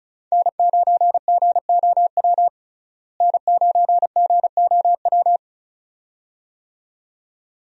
Morse code, heard as a keyed tone, about 34 wpm, 700 Hz. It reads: N9GOW N9GOW
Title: USA Call Signs - 5 Characters - Repeated 2x - 35wpm